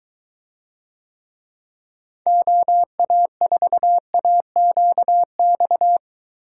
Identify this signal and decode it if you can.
Morse code, heard as a keyed tone, about 22 wpm, 700 Hz. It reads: OA4AQX